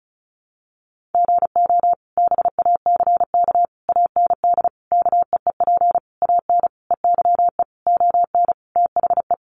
Morse code, heard as a keyed tone, about 34 wpm, 700 Hz. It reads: GO BACK AND KEEP AN EYE ON THE
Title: Sentences from Top 300 Words 35wpm